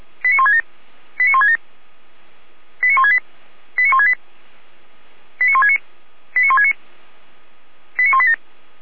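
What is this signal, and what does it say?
Five-tone selective call (ZVEI), heard as quick runs of five tones, tones of about 80 ms. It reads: 89167 89167 89168 89168 89169 89169 89187
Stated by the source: ZVEI five-tone selective call received with an SDR.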